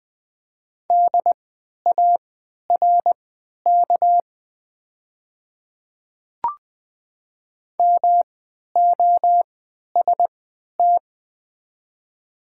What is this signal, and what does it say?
Morse code, heard as a keyed tone, about 20 wpm, spaced as for 11 wpm, 700 Hz. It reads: DARK MOST
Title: 3x Character Spacing - Top 500 Words - 20wpm